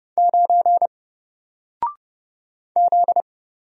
Morse code, heard as a keyed tone, about 30 wpm, 700 Hz. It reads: 9 Z
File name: Single-Letter-Number-30wpm